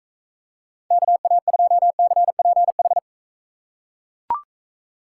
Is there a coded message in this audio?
Yes: Morse code from 0.9 to 3.0 s, heard as a keyed tone, about 44 wpm, 700 Hz. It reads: KA2CPH